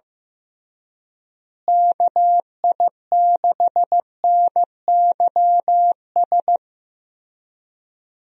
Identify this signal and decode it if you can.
Morse code, heard as a keyed tone, about 15 wpm, 700 Hz. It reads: KI6NYS
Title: USA Call Signs - 6 Characters - Repeated 3x - 15wpm